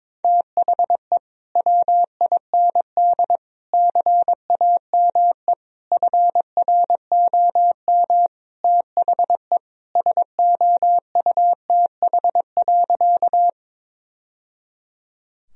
Morse code, heard as a keyed tone, about 22 wpm, 700 Hz. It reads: THE WIND CAME FROM THE SOUTH.
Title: Sentences from Top 1000 Words - Review Part 2 - 22wpm